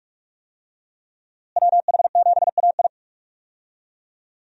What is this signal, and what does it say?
Morse code, heard as a keyed tone, about 46 wpm, 700 Hz. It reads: WH7AI
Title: USA Call Signs - 5 Characters - Repeated 2x - 45wpm